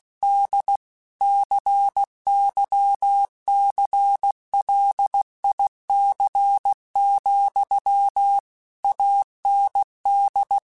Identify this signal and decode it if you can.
Morse code, heard as a keyed tone, about 16 wpm, 800 Hz. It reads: D CYCLIC, AND